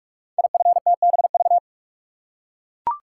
Morse code, heard as a keyed tone, about 44 wpm, 690 Hz. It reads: IFTBV
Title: Code Groups - 5 Letters - 45wpm